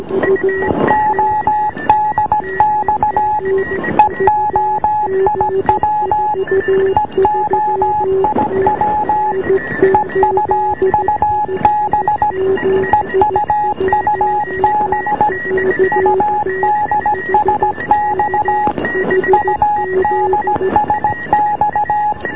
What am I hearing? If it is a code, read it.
Morse code, heard as a keyed tone, about 17 wpm, 840 Hz, with noise. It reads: ODX EOIW EOIW EUUB EUUB UDSX UDSX